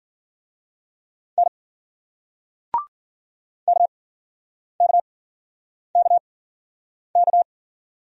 Morse code, heard as a keyed tone, about 56 wpm, spaced as for 9 wpm, 700 Hz. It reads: N KKKK